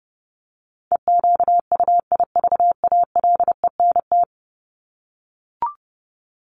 Morse code, heard as a keyed tone, about 30 wpm, 700 Hz. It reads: EQUIVALENT
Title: Top 1501-1750 Words 30wpm